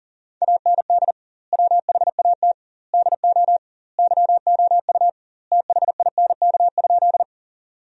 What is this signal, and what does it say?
Morse code, heard as a keyed tone, about 37 wpm, 700 Hz. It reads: AND WHAT DO YOU THINK?
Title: 2x Word-Spacing - Sentences from Top 200 Words - 40wpm